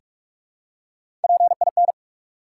Morse code, heard as a keyed tone, about 44 wpm, 700 Hz. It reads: PIN
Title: Three Letter Words - 45wpm